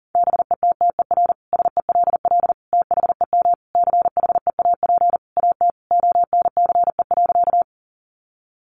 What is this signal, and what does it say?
Morse code, heard as a keyed tone, about 40 wpm, 700 Hz. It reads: BETTER SELL THEM CHEAP AT ONCE.
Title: Sentences from Top 1200 Words 40wpm